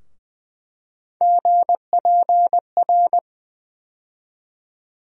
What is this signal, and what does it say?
Morse code, heard as a keyed tone, about 20 wpm, 700 Hz. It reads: GPR
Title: USA Call Sign Postfixes - 3 Letters 20wpm